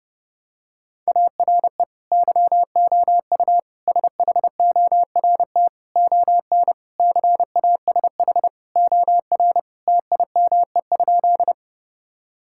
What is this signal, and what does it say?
Morse code, heard as a keyed tone, about 31 wpm, 700 Hz. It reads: ARE YOU SHORT ON CASH OR TIME?
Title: Sentences from Top 1000 Words - Review Part 5 - 30wpm